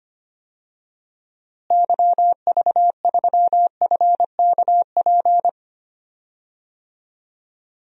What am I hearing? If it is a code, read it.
Morse code, heard as a keyed tone, about 25 wpm, 700 Hz. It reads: YV3FKP
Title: DXCC-100 Call Signs - 6 Characters - Repeated 2x - 25wpm